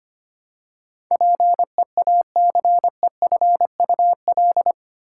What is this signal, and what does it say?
Morse code, heard as a keyed tone, about 25 wpm, 700 Hz. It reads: PEACEFUL
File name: Eight Letter Words - 25wpm